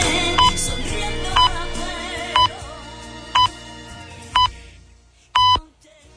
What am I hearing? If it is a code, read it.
Radio time signal (time pips). It is heard from 0.4 to 5.6 s.